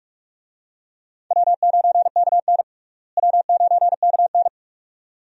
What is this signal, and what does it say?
Morse code, heard as a keyed tone, about 41 wpm, 700 Hz. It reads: W9KN W9KN
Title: USA Call Signs - 4 Characters - Repeated 2x - 45wpm